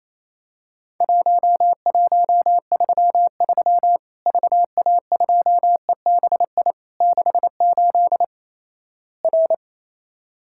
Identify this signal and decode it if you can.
Morse code, heard as a keyed tone, about 28 wpm, 700 Hz. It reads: 1133 VA2EBI 68 R
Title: Worked All Europe DX Contest - QTC Exchange - 28wpm